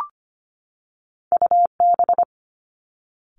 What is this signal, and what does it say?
Morse code, heard as a keyed tone, about 25 wpm, 700 Hz. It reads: UB
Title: Two Challenging Letter-Number Combinations 25wpm